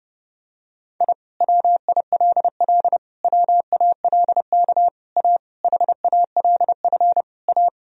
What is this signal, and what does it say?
Morse code, heard as a keyed tone, about 32 wpm, 700 Hz. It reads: I WILL WALK A HALF A
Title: Sentences from Top 1000 Words - Review Part 1 - 30wpm